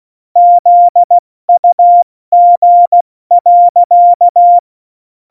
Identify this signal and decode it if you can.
Morse code, heard as a keyed tone, about 16 wpm, 700 Hz. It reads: ZUG.
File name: Morsekurs / Witze / Witz 08. Dezember 2020, 70 F80